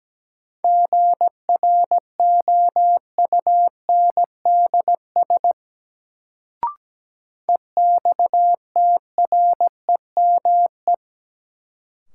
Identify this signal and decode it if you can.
Morse code, heard as a keyed tone, about 17 wpm, 700 Hz. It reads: GROUNDS EXTREME
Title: Sentences from Top 1300 Words - New Word Drill Alt - 17wpm